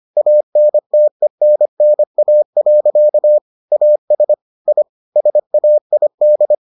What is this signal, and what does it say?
Morse code, heard as a keyed tone, about 25 wpm, 600 Hz. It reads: ANTENNA. AS I SAID